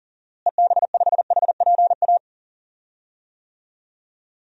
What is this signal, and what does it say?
Morse code, heard as a keyed tone, about 40 wpm, 700 Hz. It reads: EB5HPA
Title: DXCC-100 Call Signs - 6 Characters - Repeated 3x - 40wpm